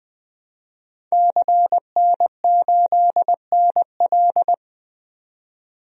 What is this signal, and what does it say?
Morse code, heard as a keyed tone, about 20 wpm, 700 Hz. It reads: CN8NL
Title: DXCC-100 Call Signs - 5 Characters - 20wpm